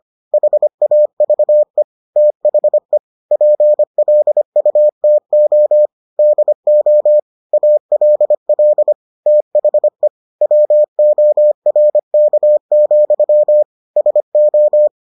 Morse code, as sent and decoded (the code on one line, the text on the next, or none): .... .- ...- . / - .... / .--. .-.. ..- - --- / -.. --- / .- .-.. .-.. / - .... / .-- --- .-. -.- --..-- / ... ---
HAVE THE PLUTO DO ALL THE WORK, SO